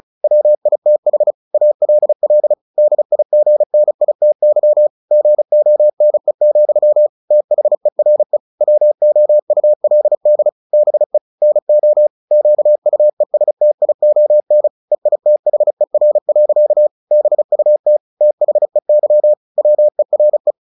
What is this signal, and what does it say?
Morse code, heard as a keyed tone, about 34 wpm, 600 Hz. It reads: WITH ALL DIGNITY GONE, THERE WOULD BE NO QUESTION EITHER. BUT THEY WERE
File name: the old man and the sea 2 35wpm